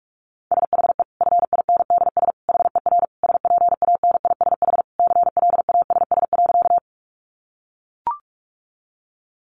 Morse code, heard as a keyed tone, about 44 wpm, 700 Hz. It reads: SHE FINDS HER SPANISH CLASS.
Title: Additional Easy Phrases 45wpm